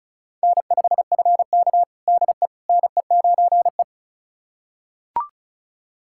Morse code, heard as a keyed tone, about 36 wpm, 700 Hz. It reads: N5FK DE NE9E
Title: USA Callsign DE Callsign - 4 Characters - 35wpm